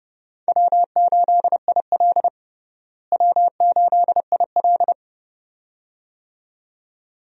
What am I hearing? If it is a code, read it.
Morse code, heard as a keyed tone, about 31 wpm, 700 Hz. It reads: W8IL W8IL
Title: USA Call Signs - 4 Characters - Repeated 2x - 30wpm